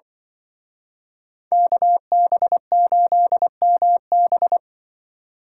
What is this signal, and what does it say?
Morse code, heard as a keyed tone, about 24 wpm, 700 Hz. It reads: KB8MB
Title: Speed Racing - USA 2x2 Call Signs 20wpm